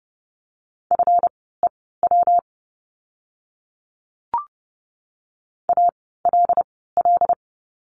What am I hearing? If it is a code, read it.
Morse code, heard as a keyed tone, about 30 wpm, spaced as for 17 wpm, 700 Hz. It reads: FEW ALL
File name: course / 30:10wpm / 237 Lesson 38f - Letter q - Words 30wpm